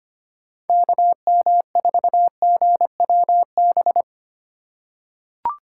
Morse code, heard as a keyed tone, about 25 wpm, 700 Hz. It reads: KM4GWB
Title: USA Call Signs - 2x Character Spacing - 25wpm